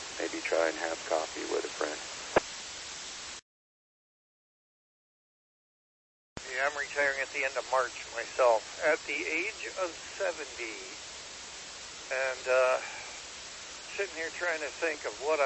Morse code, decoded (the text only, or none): none